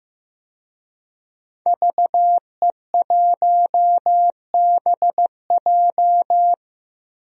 Morse code, heard as a keyed tone, about 15 wpm, 700 Hz. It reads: VE1BJ
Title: DXCC-100 Call Signs - 5 Characters - 15wpm